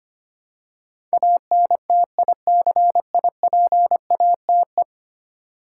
Morse code, heard as a keyed tone, about 25 wpm, 700 Hz. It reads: ANTICIPATE
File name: 10 Letter Words - First 3 Letters Given - 25wpm